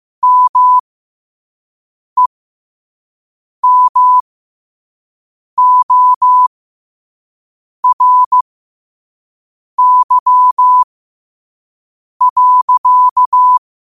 Morse code, heard as a keyed tone, about 15 wpm, spaced as for 5 wpm, 1000 Hz. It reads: MEMORY.
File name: qotdfiles / 5wpm